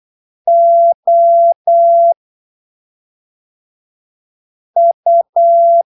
Morse code, heard as a keyed tone, about 8 wpm, spaced as for 3 wpm, 680 Hz. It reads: OU
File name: arquivos / texto / texto 13 08ppm